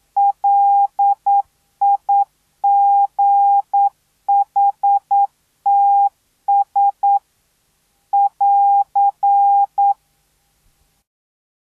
Morse code, heard as a keyed tone, about 9 wpm, 800 Hz. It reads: LIGHTS <AR>